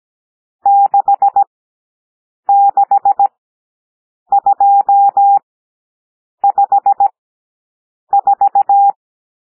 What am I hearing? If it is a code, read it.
Morse code, heard as a keyed tone, about 17 wpm, spaced as for 7 wpm, 800 Hz. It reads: 66254
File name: SOUND NUMBERS 17 wpm